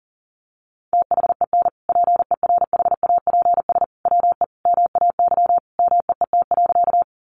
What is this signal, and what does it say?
Morse code, heard as a keyed tone, about 40 wpm, 700 Hz. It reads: THEN PERHAPS WE MAY MEET.
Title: Sentences from Top 1100 Words 40wpm